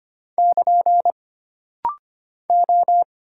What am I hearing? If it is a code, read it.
Morse code, heard as a keyed tone, about 25 wpm, 700 Hz. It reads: <KN> O
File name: Single Character (Letter:Number:Punctuation:Prosign) - Rapid-Fire - 25wpm